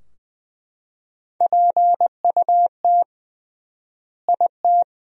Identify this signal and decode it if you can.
Morse code, heard as a keyed tone, about 21 wpm, 700 Hz. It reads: PUT IT
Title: 3x Word-Spacing - Sentences from Top 200 Words - 20wpm